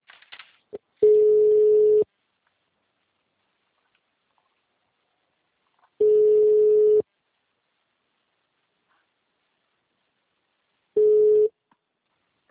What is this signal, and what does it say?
Ringback tone (Europe, 425 Hz).